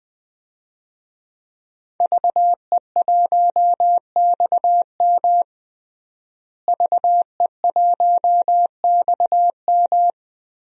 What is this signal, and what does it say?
Morse code, heard as a keyed tone, about 20 wpm, 700 Hz. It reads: VE1XM VE1XM
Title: DXCC-100 Call Signs - 5 Characters - Repeated 2x - 20wpm